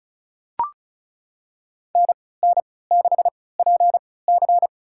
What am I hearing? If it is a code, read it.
Morse code, heard as a keyed tone, about 36 wpm, spaced as for 20 wpm, 700 Hz. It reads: NN6PC
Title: USA Call Signs - 3x Character Spacing - 35wpm